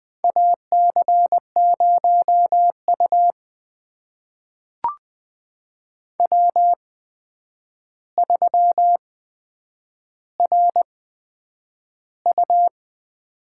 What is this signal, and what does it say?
Morse code, heard as a keyed tone, about 20 wpm, spaced as for 9 wpm, 700 Hz. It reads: AC0U W3RU
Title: USA Call Signs - 8x Character Spacing - 20wpm